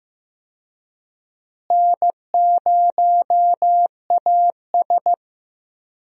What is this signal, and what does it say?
Morse code, heard as a keyed tone, about 15 wpm, 700 Hz. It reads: N0AS
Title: USA Call Signs - 4 Characters - Repeated 2x - 15wpm